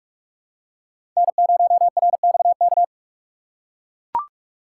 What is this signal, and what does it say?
Morse code, heard as a keyed tone, about 44 wpm, 700 Hz. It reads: N0RXK
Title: USA Call Signs - 5 Characters - 6x Character Spacing - 45wpm